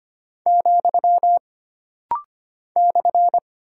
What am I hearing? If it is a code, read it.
Morse code, heard as a keyed tone, about 25 wpm, 700 Hz. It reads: , /
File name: Single Punctuation-Prosign - Rapid-Fire - 25wpm